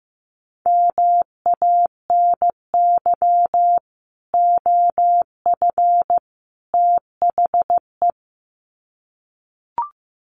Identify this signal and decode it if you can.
Morse code, heard as a keyed tone, about 15 wpm, 700 Hz. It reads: MANY OF THE
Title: Sets of 3 Words 15wpm